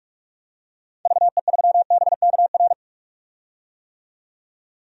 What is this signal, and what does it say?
Morse code, heard as a keyed tone, about 46 wpm, 700 Hz. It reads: VE3BKR
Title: DXCC-100 Call Signs - 6 Characters - 45wpm